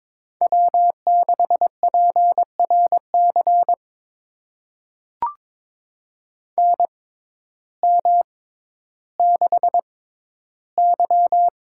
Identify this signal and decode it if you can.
Morse code, heard as a keyed tone, about 22 wpm, spaced as for 12 wpm, 700 Hz. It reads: W6PRC NM6Y
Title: USA Call Signs - 6x Character Spacing - 22wpm